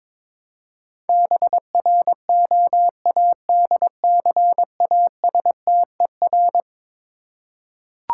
Morse code, heard as a keyed tone, about 22 wpm, 700 Hz. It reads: BROADCASTER
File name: practice / 22wpm / Eleven Letter Words - 22wpm